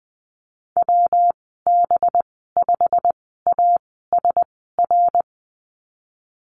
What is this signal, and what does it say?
Morse code, heard as a keyed tone, about 20 wpm, spaced as for 15 wpm, 700 Hz. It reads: WB5ASR
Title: 200 Lesson 32g - Letter k - Call Signs 20wpm